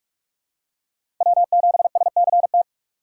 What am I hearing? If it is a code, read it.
Morse code, heard as a keyed tone, about 41 wpm, 700 Hz. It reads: W7SCT
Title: USA Call Signs - 5 Characters - 8x Character Spacing - 45wpm